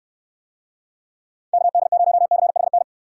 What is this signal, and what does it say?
Morse code, heard as a keyed tone, about 69 wpm, 700 Hz. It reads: KD0CHN